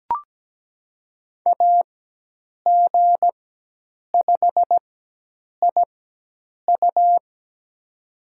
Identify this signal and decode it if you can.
Morse code, heard as a keyed tone, about 17 wpm, spaced as for 8 wpm, 700 Hz. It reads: AG5IU
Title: USA Call Signs - 4x Character Spacing - 17wpm